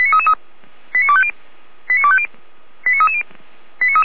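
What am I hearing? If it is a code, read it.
Five-tone selective call (ZVEI), heard as quick runs of five tones, tones of about 70 ms. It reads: 89250 89250 89330
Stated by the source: ZVEI five-tone selective call received with an SDR.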